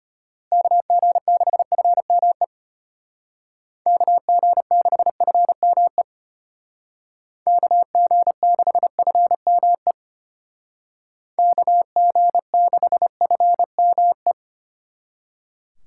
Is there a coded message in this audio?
Yes: Morse code from 0.5 to 14.3 s, heard as a keyed tone, about 31 wpm, 700 Hz. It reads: KG6FME KG6FME KG6FME KG6FME